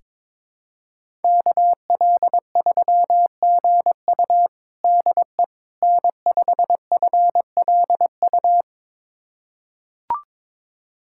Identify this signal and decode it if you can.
Morse code, heard as a keyed tone, about 22 wpm, 700 Hz. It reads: KL3GU DE N5FLU
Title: USA Callsign DE Callsign - 5 Characters - 22wpm